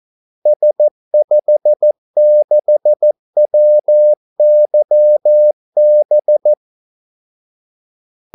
Morse code, heard as a keyed tone, about 14 wpm, 600 Hz. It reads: S56WYB
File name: Callsigns-14wpm